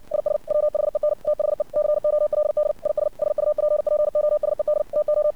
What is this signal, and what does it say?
Morse code, heard as a keyed tone, about 60 wpm, 600 Hz. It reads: IS WHEN THE COLD IS SUPPOSED TO
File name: archived / CRSnet